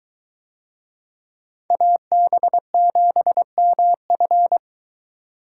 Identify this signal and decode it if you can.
Morse code, heard as a keyed tone, about 21 wpm, 700 Hz. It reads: AB7MF